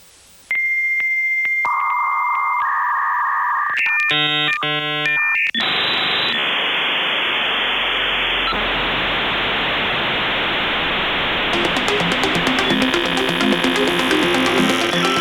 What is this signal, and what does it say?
Dial-up modem handshake.